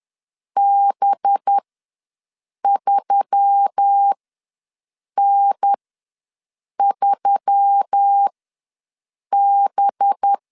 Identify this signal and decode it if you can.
Morse code, heard as a keyed tone, about 10 wpm, spaced as for 6 wpm, 800 Hz. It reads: B3N3B
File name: LEZIONI / 5 LBVC / 5 LB ws 10wpm 3616